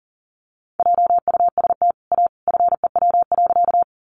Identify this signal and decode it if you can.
Morse code, heard as a keyed tone, about 40 wpm, 700 Hz. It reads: JUST A FEW.
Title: Sentences from Top 300 Words 40wpm